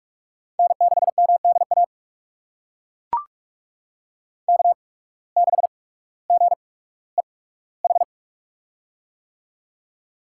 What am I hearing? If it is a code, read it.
Morse code, heard as a keyed tone, about 42 wpm, spaced as for 19 wpm, 700 Hz. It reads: N6MDA K6GEH
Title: USA Call Signs - 5 Characters - 8x Character Spacing - 45wpm